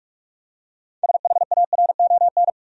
Morse code, heard as a keyed tone, about 46 wpm, 700 Hz. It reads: SHARON
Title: USA Common Names - 45wpm